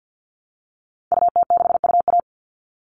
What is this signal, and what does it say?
Morse code, heard as a keyed tone, about 49 wpm, 700 Hz. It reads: UT6UA
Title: Call Signs Repeated 3x - 50wpm